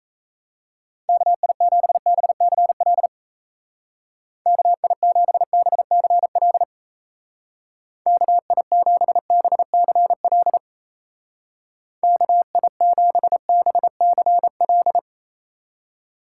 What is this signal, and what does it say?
Morse code, heard as a keyed tone, about 34 wpm, 700 Hz. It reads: KI7BCL KI7BCL KI7BCL KI7BCL
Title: Speed Racing - USA 2x3 Call Signs 28wpm